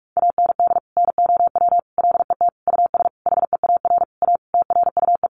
Morse code, heard as a keyed tone, about 44 wpm, 700 Hz. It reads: AND NOW LET US HEAR A TRUE
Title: Sentences from Top 900 Words 45wpm